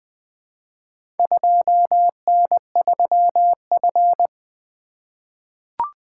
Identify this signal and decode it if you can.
Morse code, heard as a keyed tone, about 20 wpm, 700 Hz. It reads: 2N3F
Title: Four Letter-Number Combinations - 20wpm